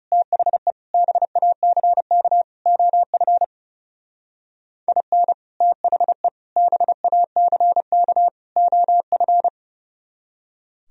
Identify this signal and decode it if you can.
Morse code, heard as a keyed tone, about 37 wpm, 700 Hz. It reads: THE BACK OF IN THE BACK OF
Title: Speed-Racing - Sets of 4 Words 30wpm